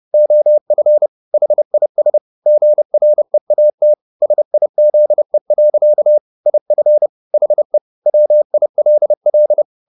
Morse code, heard as a keyed tone, about 30 wpm, 600 Hz. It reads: OF HIS GREAT SIZE. IF HE WILL
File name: the old man and the sea / the old man and the sea 1 30wpm